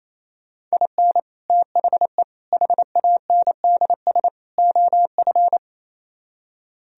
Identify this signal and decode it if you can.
Morse code, heard as a keyed tone, about 35 wpm, 700 Hz. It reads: IN THE HANDS OF
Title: Speed-Racing - Sets of 4 Words 28wpm